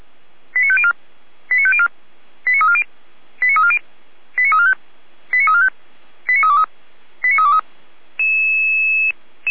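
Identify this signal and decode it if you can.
Five-tone selective call (ZVEI), heard as quick runs of five tones, tones of about 70 ms. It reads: 89504 89504 89340 89340 89345 89345 89323 89323
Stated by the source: ZVEI five-tone selective call received by SDR.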